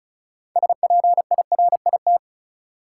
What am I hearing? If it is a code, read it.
Morse code, heard as a keyed tone, about 36 wpm, 700 Hz. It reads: SPIRIT
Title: Six Letter Words - 35wpm